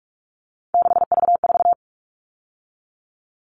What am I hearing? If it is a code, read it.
Morse code, heard as a keyed tone, about 44 wpm, 700 Hz. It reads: 6V4
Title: All Cut Numbers - 3-Digit 45wpm